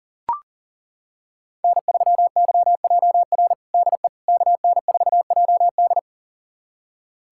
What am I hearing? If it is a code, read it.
Morse code, heard as a keyed tone, about 40 wpm, 700 Hz. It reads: N3YJR DE KN4JD
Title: USA Callsign DE Callsign - 5 Characters - 40wpm